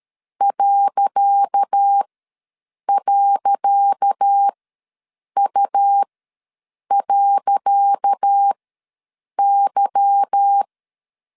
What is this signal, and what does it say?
Morse code, heard as a keyed tone, about 13 wpm, spaced as for 7 wpm, 800 Hz. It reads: ..U.Y
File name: LEZIONI / 10 SIGNS / 10C DOT ws 12wpm 3548